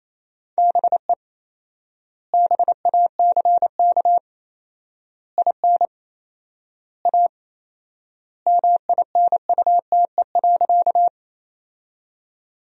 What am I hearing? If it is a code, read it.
Morse code, heard as a keyed tone, about 28 wpm, 700 Hz. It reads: BE BACK IN A MINUTE.
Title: 4x Word-Spacing - Sentences from Top 500 Words - 28wpm